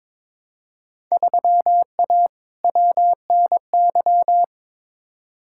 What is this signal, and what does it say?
Morse code, heard as a keyed tone, about 22 wpm, 700 Hz. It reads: 3A WNY
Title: Field Day Exchanges - 22wpm